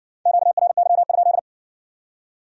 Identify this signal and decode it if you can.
Morse code, heard as a keyed tone, about 53 wpm, 700 Hz. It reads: QRY?